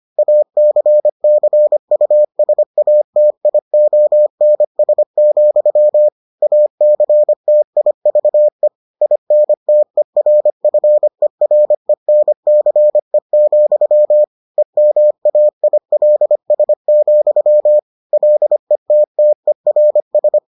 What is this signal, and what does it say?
Morse code, heard as a keyed tone, about 25 wpm, 600 Hz. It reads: ACCUSATIONS, ACTIVE INTERFERENCE, EMAILS, LETTERS